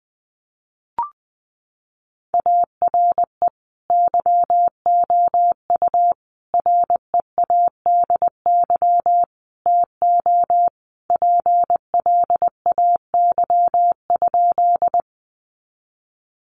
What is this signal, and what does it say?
Morse code, heard as a keyed tone, about 20 wpm, 700 Hz. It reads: ARE YOU READY TO PLAY?
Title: Sentences from Top 400 Words 20wpm